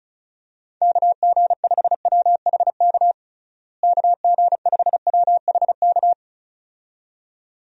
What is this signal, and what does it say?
Morse code, heard as a keyed tone, about 36 wpm, 700 Hz. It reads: KG5WHK KG5WHK
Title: USA Call Signs - 6 Characters - Repeated 2x - 35wpm